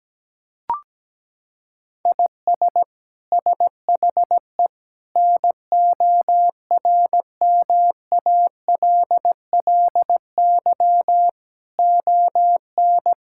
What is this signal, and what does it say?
Morse code, heard as a keyed tone, about 17 wpm, 700 Hz. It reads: IS SHE NORMALLY ON
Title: Sentences from Top 1000 Words - Review Part 6 - 17wpm